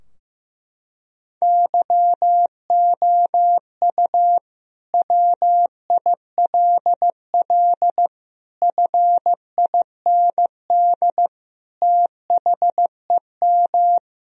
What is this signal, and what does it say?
Morse code, heard as a keyed tone, about 16 wpm, 700 Hz. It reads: YOU WILL FIND THEM
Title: Sentences from Top 1000 Words - Review Part 4 - 15wpm